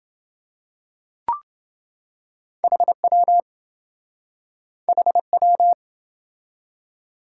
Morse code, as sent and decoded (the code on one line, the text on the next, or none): .... .-- / .... .--
HW HW